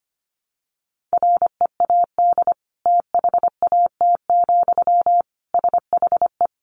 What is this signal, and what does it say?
Morse code, heard as a keyed tone, about 25 wpm, 700 Hz. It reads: READ THAT, SHE
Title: Sentences from Top 200 Words 25wpm